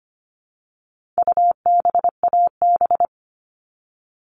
Morse code, heard as a keyed tone, about 25 wpm, 700 Hz. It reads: UBAB
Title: All Cut Numbers - 4-Digit 25wpm